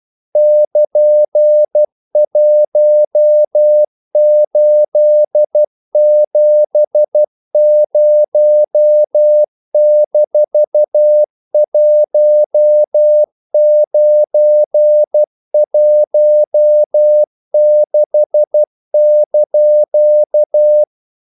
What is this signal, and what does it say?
Morse code, heard as a keyed tone, about 12 wpm, 600 Hz. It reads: <KN>1870-1916)
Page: English quotations in CW (telegraphy - morse)